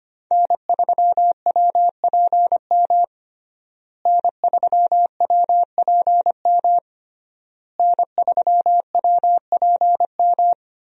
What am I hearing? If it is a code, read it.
Morse code, heard as a keyed tone, about 25 wpm, 700 Hz. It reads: N3WPM N3WPM N3WPM
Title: USA Call Signs - 5 Characters - Repeated 3x - 25wpm